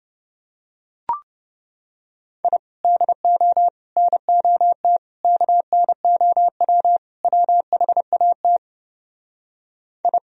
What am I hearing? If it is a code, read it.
Morse code, heard as a keyed tone, about 30 wpm, 700 Hz. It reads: I DO NOT KNOW WHAT I